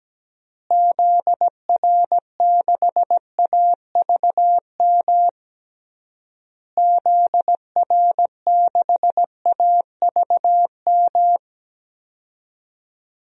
Morse code, heard as a keyed tone, about 17 wpm, 700 Hz. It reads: ZR6AVM ZR6AVM
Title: DXCC-100 Call Signs - 6 Characters - Repeated 3x - 17wpm